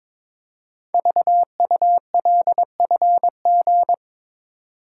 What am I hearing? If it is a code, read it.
Morse code, heard as a keyed tone, about 23 wpm, 700 Hz. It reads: VULFG
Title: Code Groups - 5 Letters - 22wpm